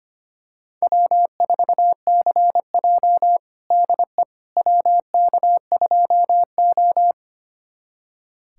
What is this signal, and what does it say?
Morse code, heard as a keyed tone, about 25 wpm, 700 Hz. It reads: W4CJ DE WK2O
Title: USA Callsign DE Callsign - 4 Characters - 25wpm